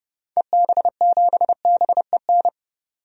Morse code, heard as a keyed tone, about 30 wpm, 700 Hz. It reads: EB7BEN
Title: DXCC-100 Call Signs - 6 Characters - 30wpm